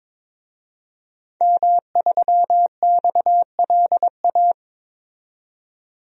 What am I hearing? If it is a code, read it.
Morse code, heard as a keyed tone, about 22 wpm, 700 Hz. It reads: M3XLA
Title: DXCC-100 Call Signs - 5 Characters - 22wpm